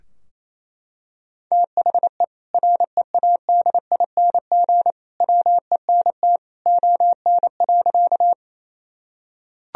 Morse code, heard as a keyed tone, about 28 wpm, 700 Hz. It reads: THE READING WENT ON.
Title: Sentences from Top 1000 Words - Review Part 3 - 28wpm